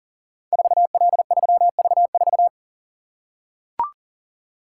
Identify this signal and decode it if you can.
Morse code, heard as a keyed tone, about 42 wpm, 690 Hz. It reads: 4L3V4
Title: Code Groups - 5 Letter:Numbers - 40wpm